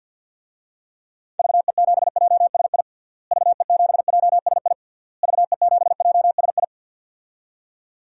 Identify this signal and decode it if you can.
Morse code, heard as a keyed tone, about 50 wpm, 700 Hz. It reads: VE7JSI VE7JSI VE7JSI
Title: DXCC-100 Call Signs - 6 Characters - Repeated 3x - 50wpm